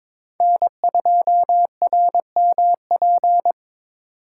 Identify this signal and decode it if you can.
Morse code, heard as a keyed tone, about 22 wpm, 700 Hz. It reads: N2RMP
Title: USA Call Signs - 5 Characters - 3x Character Spacing - 22wpm